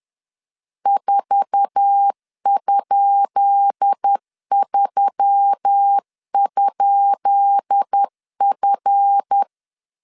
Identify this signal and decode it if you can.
Morse code, heard as a keyed tone, about 10 wpm, 800 Hz. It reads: 4?3?F
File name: LEZIONI / 10 SIGNS / 10B INT 10wpm 2844